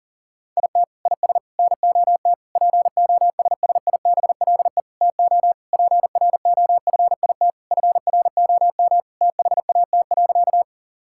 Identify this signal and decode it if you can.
Morse code, heard as a keyed tone, about 40 wpm, 700 Hz. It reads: IT IS NOT POSSIBLE TO PROFIT FROM THAT.
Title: Sentences from Top 1000 Words - Review Part 5 - 40wpm